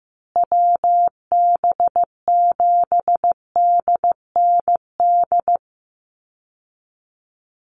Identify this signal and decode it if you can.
Morse code, heard as a keyed tone, about 14 wpm, 700 Hz. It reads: WB7DND
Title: Call Signs Repeated 2x - 15wpm